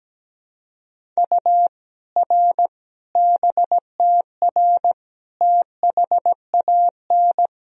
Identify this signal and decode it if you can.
Morse code, heard as a keyed tone, about 17 wpm, 700 Hz. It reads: U R BTR THAN
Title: Sentences from Top 400 Words and Abbreviations 17wpm